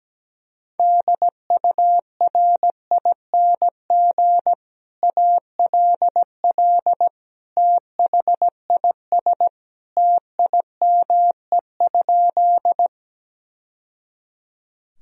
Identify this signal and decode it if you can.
Morse code, heard as a keyed tone, about 17 wpm, 700 Hz. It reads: DURING ALL THIS TIME?
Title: Sentences from Top 1000 Words - Review Part 2 - 17wpm